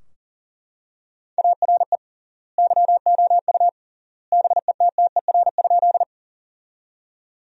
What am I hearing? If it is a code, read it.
Morse code, heard as a keyed tone, about 40 wpm, 700 Hz. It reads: ARE YOU BETTER?
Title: 3x Word-Spacing - Sentences from Top 400 Words - 40wpm